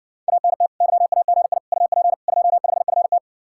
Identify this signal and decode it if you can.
Morse code, heard as a keyed tone, about 70 wpm, 700 Hz. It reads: ANT YAGI UP 25FT